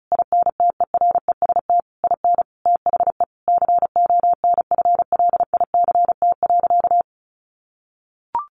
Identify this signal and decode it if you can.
Morse code, heard as a keyed tone, about 34 wpm, 700 Hz. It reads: INTEREST IN THE CONFLICT.
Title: Sentences from Top 1300 Words 35wpm